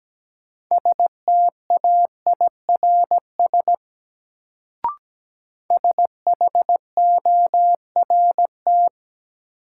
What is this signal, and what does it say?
Morse code, heard as a keyed tone, about 17 wpm, 700 Hz. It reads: STAIRS SHORT
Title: Sets of 5 Words Encore - New Word Drill - 17wpm